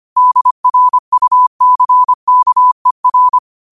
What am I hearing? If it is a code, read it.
Morse code, heard as a keyed tone, about 25 wpm, 1000 Hz. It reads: DRUCKER
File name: qotdfiles / 25wpm